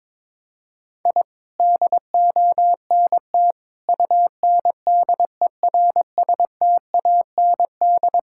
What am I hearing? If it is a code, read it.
Morse code, heard as a keyed tone, about 21 wpm, 700 Hz. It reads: I DONT UNDERSTAND
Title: Sentences from Top 1000 Words - Review Part 5 - 22wpm